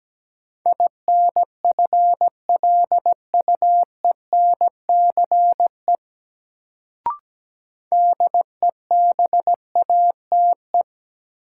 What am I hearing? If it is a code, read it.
Morse code, heard as a keyed tone, about 17 wpm, 700 Hz. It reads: INFLUENCE DEBATE
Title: Sentences from Top 1100 Words - New Word Drill - 17wpm